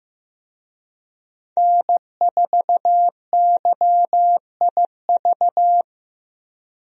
Morse code, heard as a keyed tone, about 14 wpm, 700 Hz. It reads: N4YIV
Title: USA Call Signs - 5 Characters - Repeated 2x - 15wpm